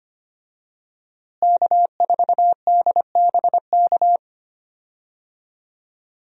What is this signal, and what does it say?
Morse code, heard as a keyed tone, about 25 wpm, 700 Hz. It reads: K4DBK
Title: USA Call Signs - 5 Characters - Repeated 2x - 25wpm